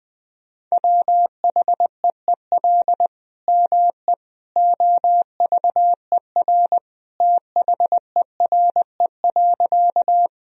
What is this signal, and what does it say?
Morse code, heard as a keyed tone, about 20 wpm, 700 Hz. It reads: WHEEL ME OVER THERE.
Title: Sentences from Top 1000 Words - Review Part 3 - 20wpm